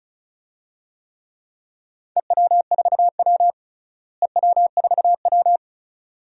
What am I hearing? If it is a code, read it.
Morse code, heard as a keyed tone, about 36 wpm, 700 Hz. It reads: EW4W EW4W
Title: DXCC-100 Call Signs - 4 Characters - Repeated 3x - 35wpm